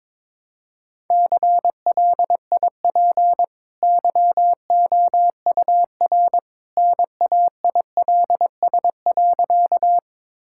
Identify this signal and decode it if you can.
Morse code, heard as a keyed tone, about 22 wpm, 700 Hz. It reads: CLIP YOUR NAILS.
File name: practice / 22wpm / Easy Sentences with Hints - 22wpm